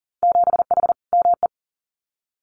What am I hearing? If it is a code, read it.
Morse code, heard as a keyed tone, about 42 wpm, 700 Hz. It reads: 7H ME
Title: Winter Field Day Exchanges Encore 40wpm